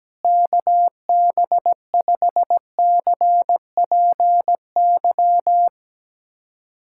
Morse code, heard as a keyed tone, about 17 wpm, 700 Hz. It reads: KB5CPY
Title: USA Callsign DE Callsign 17wpm